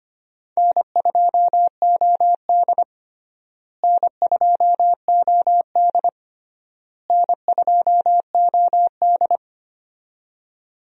Morse code, heard as a keyed tone, about 25 wpm, 700 Hz. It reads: N2OD N2OD N2OD
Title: USA Call Signs - 4 Characters - Repeated 3x - 25wpm